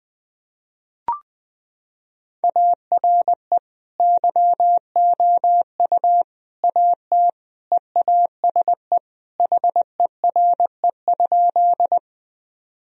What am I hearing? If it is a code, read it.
Morse code, heard as a keyed tone, about 20 wpm, 700 Hz. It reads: ARE YOU AT EASE HERE?
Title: Sentences from Top 1000 Words - Review Part 1 - 20wpm